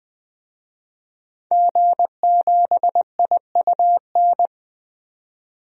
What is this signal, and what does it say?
Morse code, heard as a keyed tone, about 20 wpm, 700 Hz. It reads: G7IUN
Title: DXCC-100 Call Signs - 5 Characters - 20wpm